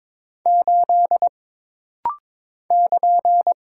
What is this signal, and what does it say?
Morse code, heard as a keyed tone, about 22 wpm, 700 Hz. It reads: 8 <KN>
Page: This rapid-fire timing is geared to push your ICR (Instant Character Recognition) skills.